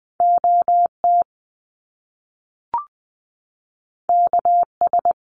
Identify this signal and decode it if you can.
Morse code, heard as a keyed tone, about 20 wpm, 700 Hz. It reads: OT KS
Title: Top Two Letter Combinations in Words 20wpm